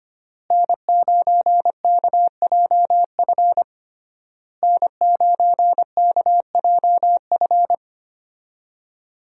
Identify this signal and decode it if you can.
Morse code, heard as a keyed tone, about 25 wpm, 700 Hz. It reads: N9KJF N9KJF
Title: USA Call Signs - 5 Characters - Repeated 3x - 25wpm